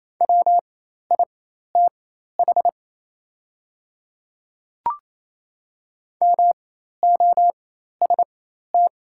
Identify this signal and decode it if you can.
Morse code, heard as a keyed tone, about 27 wpm, spaced as for 13 wpm, 700 Hz. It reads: WITH MOST
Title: 4x Character Spacing - Top 100 Words - 28wpm